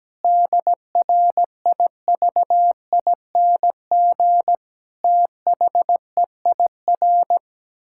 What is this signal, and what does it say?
Morse code, heard as a keyed tone, about 17 wpm, 700 Hz. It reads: DRIVING THEIR
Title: Pangram Challenge v2 - 17wpm